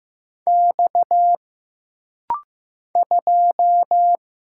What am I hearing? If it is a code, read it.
Morse code, heard as a keyed tone, about 15 wpm, 700 Hz. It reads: X 2
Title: Single Letter-Number Alt - Rapid-Fire 15wpm